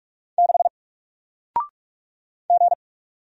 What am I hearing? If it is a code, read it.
Morse code, heard as a keyed tone, about 44 wpm, 700 Hz. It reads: 6 G
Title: Single Character (Letter:Number:Punctuation:Prosign) Alt 45wpm